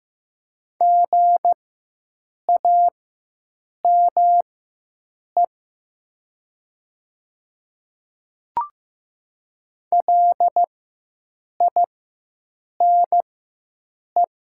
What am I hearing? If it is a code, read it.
Morse code, heard as a keyed tone, about 15 wpm, spaced as for 7 wpm, 700 Hz. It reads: GAME LINE
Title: 4x Character Spacing - Top 500 Words - 15wpm